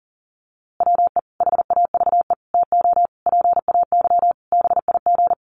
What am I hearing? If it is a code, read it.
Morse code, heard as a keyed tone, about 37 wpm, 700 Hz. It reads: WE HAVE TO PAY BIG
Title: A Sound Of Thunder 40wpm